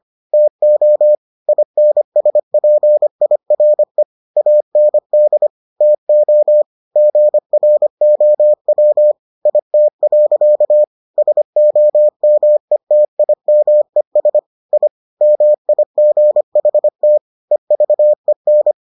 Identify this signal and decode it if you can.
Morse code, heard as a keyed tone, about 25 wpm, 600 Hz. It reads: TO INSPIRE AND TO GROW IT. SOMETIMES I MIGHT EVEN